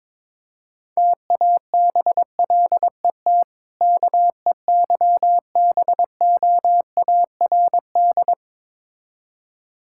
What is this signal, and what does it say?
Morse code, heard as a keyed tone, about 22 wpm, 700 Hz. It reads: TABLET KEYBOARD
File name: Difficult Words with Hints - 22wpm